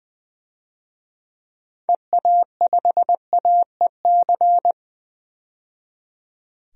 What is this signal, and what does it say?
Morse code, heard as a keyed tone, about 20 wpm, 700 Hz. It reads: EA5AEC
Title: DXCC-100 Call Signs - 6 Characters - 20wpm